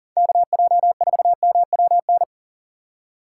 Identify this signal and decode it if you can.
Morse code, heard as a keyed tone, about 49 wpm, 700 Hz. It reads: KJ4MWN
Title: Speed Racing - USA Call Signs 40wpm